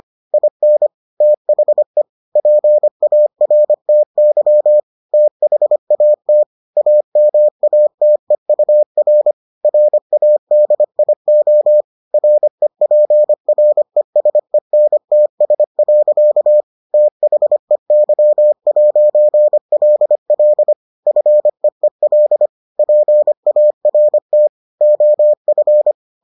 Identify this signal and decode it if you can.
Morse code, heard as a keyed tone, about 25 wpm, 600 Hz. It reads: IN THE PARTY THAT AMATEUR RADIO REPRESENTS. THEY'LL FEEL PART OF